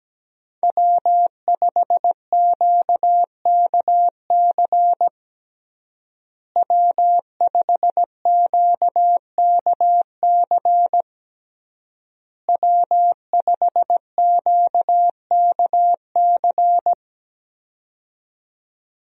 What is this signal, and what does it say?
Morse code, heard as a keyed tone, about 17 wpm, 700 Hz. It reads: W5QKC W5QKC W5QKC
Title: USA Call Signs - 5 Characters - Repeated 3x - 17wpm